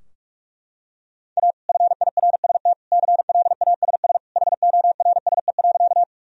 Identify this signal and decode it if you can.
Morse code, heard as a keyed tone, about 46 wpm, 700 Hz. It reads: A FIRST CLASS HORSE.
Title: Sentences from Top 1000 Words - Review Part 2 - 45wpm